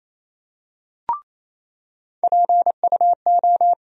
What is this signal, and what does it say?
Morse code, heard as a keyed tone, about 28 wpm, 700 Hz. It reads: PUO
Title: USA Call Sign Postfixes - 3 Letters 28wpm